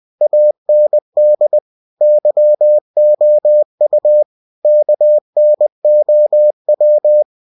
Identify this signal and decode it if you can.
Morse code, heard as a keyed tone, about 20 wpm, 600 Hz. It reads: AND YOU KNOW
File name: cw-books / winnie-the-pooh / winnie-20wpm-page-0072